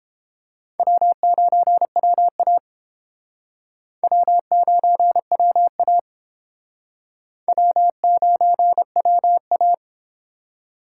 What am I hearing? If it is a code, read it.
Morse code, heard as a keyed tone, about 29 wpm, 700 Hz. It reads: W9WA W9WA W9WA